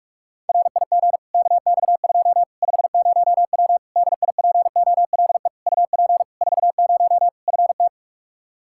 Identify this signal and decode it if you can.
Morse code, heard as a keyed tone, about 41 wpm, 700 Hz. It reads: RIG KX2 50W DIPOLE UP 40 FT
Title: Practice QSO - Exchange 2 - Info - 45wpm